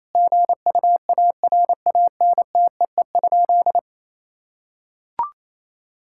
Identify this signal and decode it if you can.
Morse code, heard as a keyed tone, about 28 wpm, 700 Hz. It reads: GUARANTEE?
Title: Sentences from Top 1600 Words 28wpm